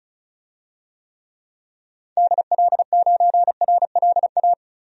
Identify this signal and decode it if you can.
Morse code, heard as a keyed tone, about 34 wpm, 700 Hz. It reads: DL9RLA
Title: DXCC-100 Call Signs - 6 Characters - Repeated 3x - 35wpm